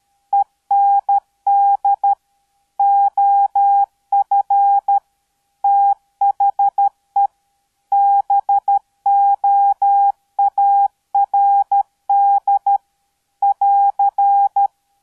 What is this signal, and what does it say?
Morse code, heard as a keyed tone, about 13 wpm, 800 Hz. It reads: END OF THE BOARD <AR>